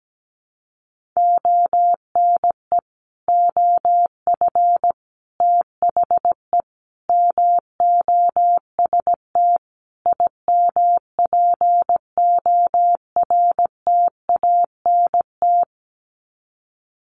Morse code, heard as a keyed tone, about 17 wpm, 700 Hz. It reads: ONE OF THE MOST IMPORTANT
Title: Sets of 5 Words 17wpm